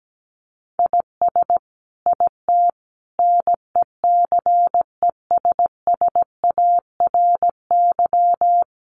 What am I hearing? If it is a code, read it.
Morse code, heard as a keyed tone, about 17 wpm, 700 Hz. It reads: IS IT NECESSARY
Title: Sentences from Top 1500 Words 17wpm